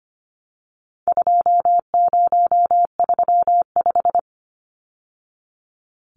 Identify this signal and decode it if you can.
Morse code, heard as a keyed tone, about 25 wpm, 700 Hz. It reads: 2035
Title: Numbers - 4-Digit 25wpm